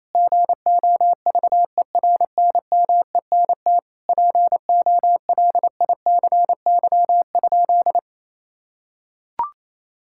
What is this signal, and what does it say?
Morse code, heard as a keyed tone, about 28 wpm, 700 Hz. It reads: GOVERNMENT POLICY?
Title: Sentences from Top 1000 Words - Review Part 5 - 28wpm